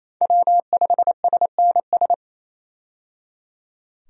Morse code, heard as a keyed tone, about 28 wpm, 700 Hz. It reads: W5SNS
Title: USA Call Signs - 5 Characters - 28wpm